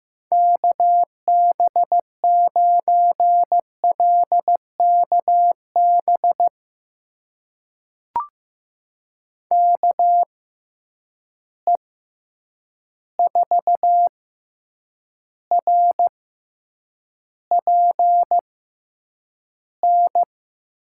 Morse code, heard as a keyed tone, about 15 wpm, spaced as for 8 wpm, 700 Hz. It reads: KB9LKB KE4RPN
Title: USA Call Signs - 6 Characters - 6x Character Spacing - 15wpm